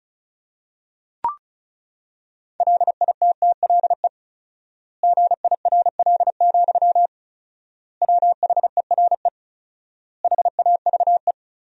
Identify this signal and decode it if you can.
Morse code, heard as a keyed tone, about 36 wpm, 700 Hz. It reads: LITTLE GIRL, WHERE HAVE
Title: 4x Word-Spacing - Sentences from Top 400 Words - 35wpm